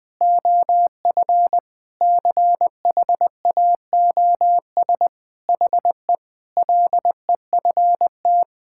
Morse code, heard as a keyed tone, about 20 wpm, 700 Hz. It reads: OF CHAOS HE LEFT
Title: Sentences with 4-Word Phrases - 20wpm